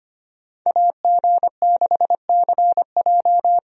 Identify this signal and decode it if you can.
Morse code, heard as a keyed tone, about 25 wpm, 700 Hz. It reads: AG6CJ
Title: USA Call Signs - 5 Characters - Repeated 3x - 25wpm